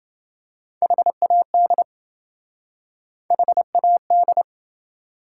Morse code, heard as a keyed tone, about 27 wpm, 700 Hz. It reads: HAD HAD